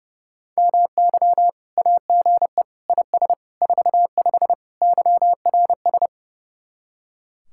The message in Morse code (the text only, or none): MY AGE IS 45 YRS